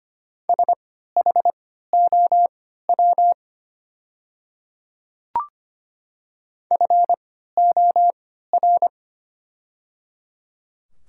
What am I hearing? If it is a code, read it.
Morse code, heard as a keyed tone, about 25 wpm, spaced as for 14 wpm, 700 Hz. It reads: SHOW FOR
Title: 3x Character Spacing - Top 200 Words - 25wpm